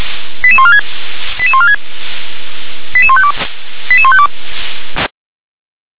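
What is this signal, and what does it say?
Five-tone selective call (ZVEI), heard as quick runs of five tones, tones of about 70 ms. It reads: 88146 88146 88152 88152
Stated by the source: Five-tone ZVEI selective call, received with an SDR.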